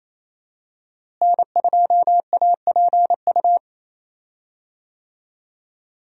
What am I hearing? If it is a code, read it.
Morse code, heard as a keyed tone, about 28 wpm, 700 Hz. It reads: N2APU
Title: USA Call Signs - 5 Characters - Repeated 2x - 28wpm